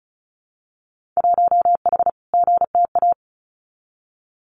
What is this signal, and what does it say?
Morse code, heard as a keyed tone, about 34 wpm, 700 Hz. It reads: 1H GTA
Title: Winter Field Day Exchanges 35wpm